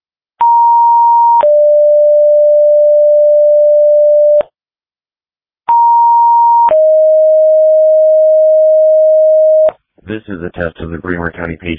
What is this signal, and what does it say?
Two-tone page, heard as a one-second tone, then a three-second tone, about 940 then 590 Hz, about 940 then 620 Hz, with speech. It is heard from 0.4 to 9.7 s.
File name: dispatch / Tripoli Fire 2026 07 09 19 06 59